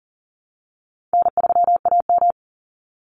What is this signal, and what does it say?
Morse code, heard as a keyed tone, about 40 wpm, 700 Hz. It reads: N3AM
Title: Easy Call Signs Repeated 3x - 40wpm